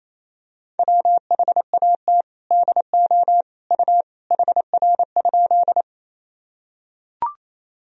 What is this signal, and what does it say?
Morse code, heard as a keyed tone, about 28 wpm, 700 Hz. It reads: WHAT DO U HR?
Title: Sentences from Top 300 Words and Abbreviations 28wpm